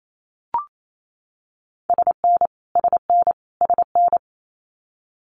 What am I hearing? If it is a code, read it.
Morse code, heard as a keyed tone, about 28 wpm, 700 Hz. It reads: SN SN SN
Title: Multi-letter Phonemes Repeated 3x 28wpm